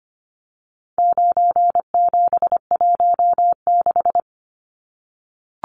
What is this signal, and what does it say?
Morse code, heard as a keyed tone, about 25 wpm, 700 Hz. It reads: 9716
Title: Numbers - 4-Digit 25wpm